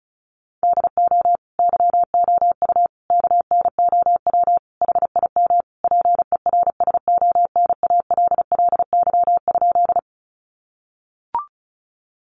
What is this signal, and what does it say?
Morse code, heard as a keyed tone, about 34 wpm, 700 Hz. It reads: DO YOU KNOW HIM PERSONALLY?
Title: Sentences from Top 1400 Words 35wpm